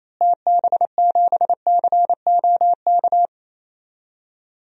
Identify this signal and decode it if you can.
Morse code, heard as a keyed tone, about 28 wpm, 700 Hz. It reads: TB7COK
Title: DXCC-100 Call Signs - 6 Characters - Repeated 3x - 28wpm